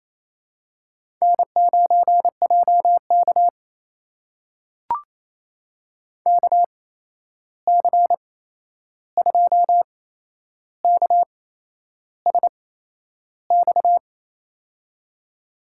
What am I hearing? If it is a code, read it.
Morse code, heard as a keyed tone, about 28 wpm, spaced as for 11 wpm, 700 Hz. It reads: N9JK KC2KSX
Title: USA Call Signs - 8x Character Spacing - 28wpm